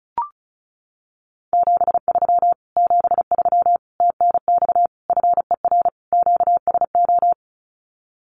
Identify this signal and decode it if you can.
Morse code, heard as a keyed tone, about 34 wpm, 700 Hz. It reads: 73 73 TNX FER QSO
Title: Common QSO Phrases Encore 35wpm